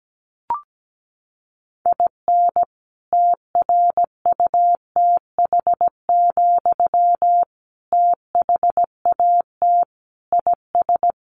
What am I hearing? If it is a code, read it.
Morse code, heard as a keyed tone, about 17 wpm, 700 Hz. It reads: IN TRUTH, THAT IS
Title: Sentences from Top 1300 Words 17wpm